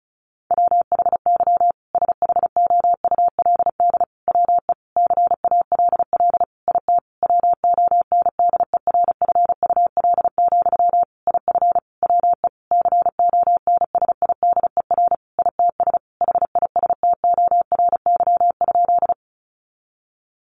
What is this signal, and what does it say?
Morse code, heard as a keyed tone, about 34 wpm, 700 Hz. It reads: WHY SHOULD WE CALL IT WONDERFUL, IF WE CONSIDER ITS HISTORY?
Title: Sentences from Top 1100 Words 35wpm